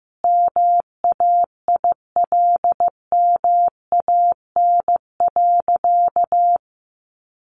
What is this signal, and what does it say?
Morse code, heard as a keyed tone, about 15 wpm, 700 Hz. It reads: MAILMAN.
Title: Easy Phrases 15wpm